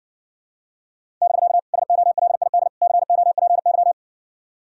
Morse code, heard as a keyed tone, about 60 wpm, 700 Hz. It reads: <BK> SOLID COPY